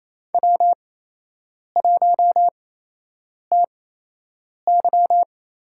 Morse code, heard as a keyed tone, about 28 wpm, spaced as for 8 wpm, 700 Hz. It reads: W1TY
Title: USA Call Signs - 4 Characters - 8x Character Spacing - 28wpm